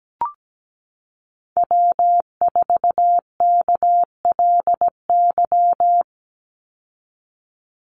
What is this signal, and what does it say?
Morse code, heard as a keyed tone, about 17 wpm, 700 Hz. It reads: W4KLY
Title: Call Signs Repeated 2x - 17wpm